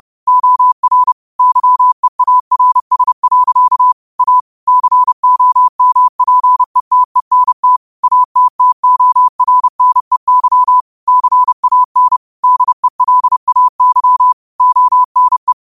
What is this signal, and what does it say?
Morse code, heard as a keyed tone, about 30 wpm, 1000 Hz. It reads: OR YEARS. A COMPETENT ATTORNEY CAN DELAY ONE
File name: qotdfiles / 30wpm